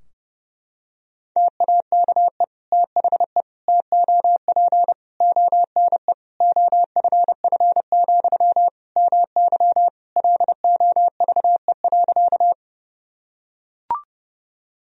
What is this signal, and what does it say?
Morse code, heard as a keyed tone, about 32 wpm, 700 Hz. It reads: TAKE THE TOP ONE OFF, MY LOVE.
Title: Sentences from Top 1000 Words - Review Part 2 - 30wpm